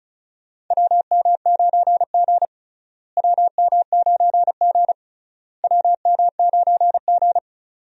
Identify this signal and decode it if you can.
Morse code, heard as a keyed tone, about 36 wpm, 700 Hz. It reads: WM9G WM9G WM9G
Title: USA Call Signs - 4 Characters - Repeated 3x - 35wpm